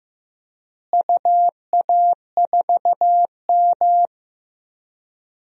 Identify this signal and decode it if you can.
Morse code, heard as a keyed tone, about 15 wpm, 700 Hz. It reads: UA4M
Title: DXCC-100 Call Signs - 4 Characters - Repeated 2x - 15wpm